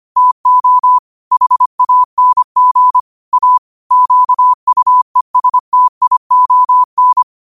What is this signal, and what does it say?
Morse code, heard as a keyed tone, about 25 wpm, 1000 Hz. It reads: TO HANG A QUESTION
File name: qotdfiles / 25wpm